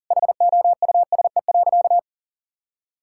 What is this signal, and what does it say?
Morse code, heard as a keyed tone, about 40 wpm, 700 Hz. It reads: HOUSE.
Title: Sentences from Top 1000 Words - Review Part 6 - 40wpm